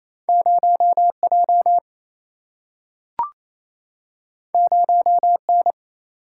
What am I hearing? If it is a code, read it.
Morse code, heard as a keyed tone, about 28 wpm, 700 Hz. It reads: 0J 0N
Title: Single-Letter Single-Number 28wpm